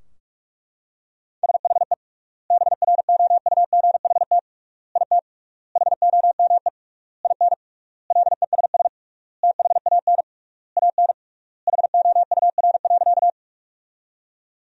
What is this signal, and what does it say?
Morse code, heard as a keyed tone, about 41 wpm, 700 Hz. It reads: SHE BROUGHT IT HOME IN LESS THAN AN HOUR.
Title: 3x Word-Spacing - Sentences from Top 500 Words - 45wpm